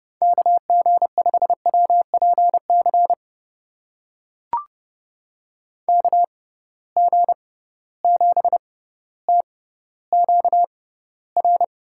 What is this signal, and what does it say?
Morse code, heard as a keyed tone, about 30 wpm, spaced as for 15 wpm, 700 Hz. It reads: KG5WPC KG7TQR
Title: USA Call Signs - 6 Characters - 6x Character Spacing - 30wpm